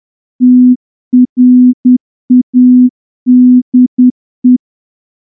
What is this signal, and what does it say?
Morse code, heard as a keyed tone, about 10 wpm, 250 Hz. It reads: TRADE
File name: morse-training-10WPM